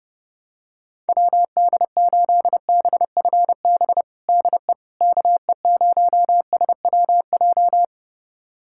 Morse code, heard as a keyed tone, about 30 wpm, 700 Hz. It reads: WD8BFB DE KE0SWJ
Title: USA Callsign DE Callsign - 6 Characters - 30wpm